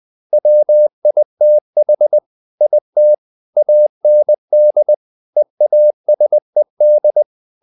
Morse code, heard as a keyed tone, about 20 wpm, 600 Hz. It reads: WITH IT AND EASED
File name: the old man and the sea part 2 20wpm